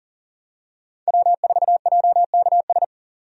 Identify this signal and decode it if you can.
Morse code, heard as a keyed tone, about 40 wpm, 700 Hz. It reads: W4JKS
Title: USA Call Signs - 5 Characters - 40wpm